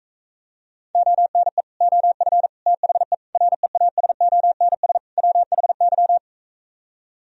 Morse code, heard as a keyed tone, about 45 wpm, 700 Hz. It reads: ONE OF THE REASONS WHY